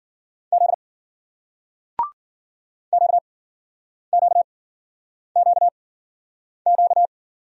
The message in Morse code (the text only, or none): Z Q Q Q Q